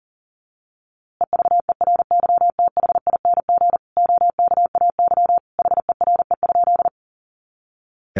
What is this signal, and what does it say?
Morse code, heard as a keyed tone, about 37 wpm, 700 Hz. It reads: EVERYTHING OKAY HERE?
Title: A Sound Of Thunder 40wpm